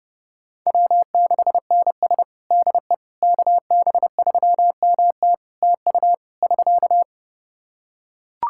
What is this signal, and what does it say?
Morse code, heard as a keyed tone, about 30 wpm, 700 Hz. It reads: W6NS DE KB3MT TU <SK>
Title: Practice QSO - Exchange 4 - Ending - 30wpm